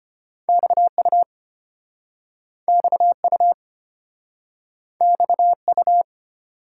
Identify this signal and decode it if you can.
Morse code, heard as a keyed tone, about 32 wpm, 700 Hz. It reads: XU XU XU